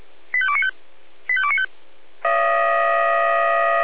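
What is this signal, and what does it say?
Five-tone selective call (ZVEI), heard as quick runs of five tones, tones of about 70 ms. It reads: 85295 85295
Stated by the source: ZVEI five-tone selective call 85242 received on an SDR.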